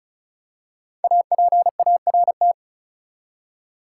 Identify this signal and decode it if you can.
Morse code, heard as a keyed tone, about 34 wpm, 700 Hz. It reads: APART
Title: Five Letter Words - 35wpm